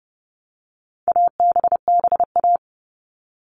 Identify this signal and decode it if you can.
Morse code, heard as a keyed tone, about 30 wpm, 700 Hz. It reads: ABBA
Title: All Cut Numbers - 4-Digit 30wpm